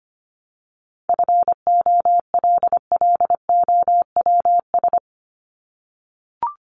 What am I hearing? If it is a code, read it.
Morse code, heard as a keyed tone, about 25 wpm, 700 Hz. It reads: FOLLOWS
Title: ES:S Suffix - Top 100 Words with Base Words 25wpm